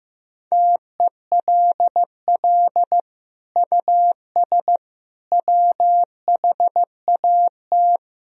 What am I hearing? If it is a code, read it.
Morse code, heard as a keyed tone, about 16 wpm, 700 Hz. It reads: TELL US WHAT
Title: Sentences from Top 1000 Words - Review Part 4 - 15wpm